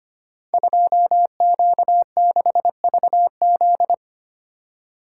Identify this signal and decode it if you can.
Morse code, heard as a keyed tone, about 25 wpm, 700 Hz. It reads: 2Q6VZ
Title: Code Groups - 5 Letter:Numbers - 25wpm